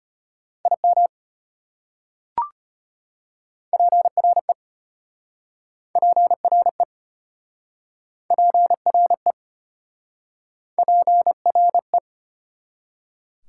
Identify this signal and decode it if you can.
Morse code, heard as a keyed tone, about 33 wpm, 700 Hz. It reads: IM PRE PRE PRE PRE